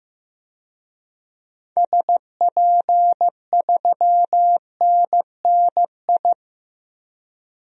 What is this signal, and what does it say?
Morse code, heard as a keyed tone, about 15 wpm, 700 Hz. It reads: SP3NNI
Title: DXCC-100 Call Signs - 6 Characters - 15wpm